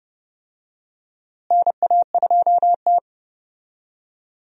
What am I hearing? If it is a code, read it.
Morse code, heard as a keyed tone, about 30 wpm, 700 Hz. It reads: NA2T
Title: USA Call Signs - 4 Characters - Repeated 2x - 30wpm